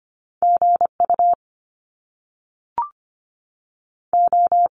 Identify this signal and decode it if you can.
Morse code, heard as a keyed tone, about 25 wpm, 700 Hz. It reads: GU O
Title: Top Two Letter Combinations in Words 25wpm